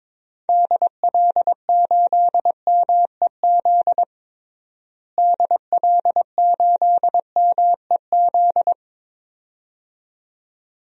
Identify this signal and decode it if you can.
Morse code, heard as a keyed tone, about 22 wpm, 700 Hz. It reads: DL8MEZ DL8MEZ
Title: DXCC-100 Call Signs - 6 Characters - Repeated 2x - 22wpm